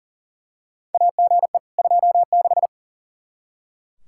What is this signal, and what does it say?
Morse code, heard as a keyed tone, about 40 wpm, 700 Hz. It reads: AGE 26
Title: Practice QSO - Exchange 3 - More Info - 40wpm